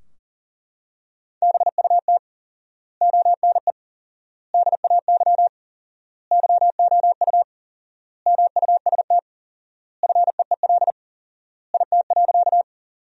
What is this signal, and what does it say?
Morse code, heard as a keyed tone, about 40 wpm, 700 Hz. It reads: BUT ONE DAY YOU MUST FEEL IT.
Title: 4x Word-Spacing - Sentences from Top 400 Words - 40wpm